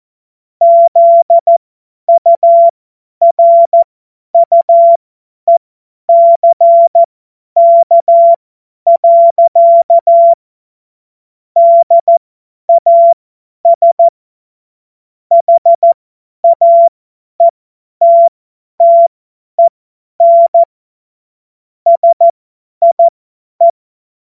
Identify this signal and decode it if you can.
Morse code, heard as a keyed tone, about 14 wpm, spaced as for 10 wpm, 680 Hz. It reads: ZURUECK. DAS HAETTEN SIE
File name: Morsekurs / Witze / Witz 20. Januar 2026 50 F70